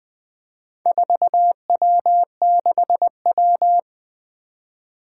Morse code, heard as a keyed tone, about 20 wpm, 700 Hz. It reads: 4W6W
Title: Four Challenging Letter-Number Combinations 20wpm